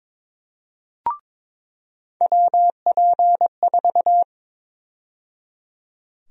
Morse code, heard as a keyed tone, about 22 wpm, 700 Hz. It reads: WP4
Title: USA Call Sign Prefixes - 22wpm